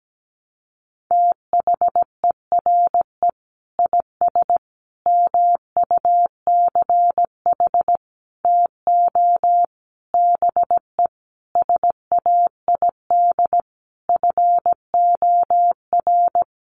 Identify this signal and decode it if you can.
Morse code, heard as a keyed tone, about 17 wpm, 700 Hz. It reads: THERE IS MUCH TO BE SAID FOR
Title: Sentences from Top 1300 Words 17wpm